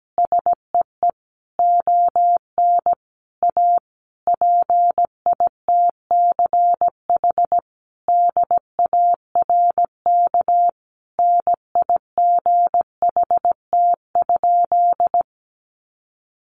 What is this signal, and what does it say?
Morse code, heard as a keyed tone, about 17 wpm, 700 Hz. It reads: SEE ON A PITCH DARK NIGHT?
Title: Sentences from Top 1300 Words 17wpm